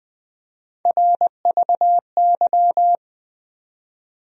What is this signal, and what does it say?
Morse code, heard as a keyed tone, about 21 wpm, 700 Hz. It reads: RVY
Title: USA Call Sign Postfixes - 3 Letters 20wpm